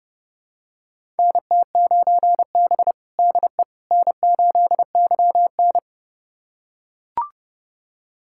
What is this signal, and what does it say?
Morse code, heard as a keyed tone, about 29 wpm, 700 Hz. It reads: NT9B DE N8YN
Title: USA Callsign DE Callsign - 4 Characters - 30wpm